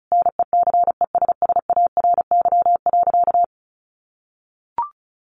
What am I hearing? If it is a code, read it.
Morse code, heard as a keyed tone, about 34 wpm, 700 Hz. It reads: NECESSARY.
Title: Sentences from Top 800 Words 35wpm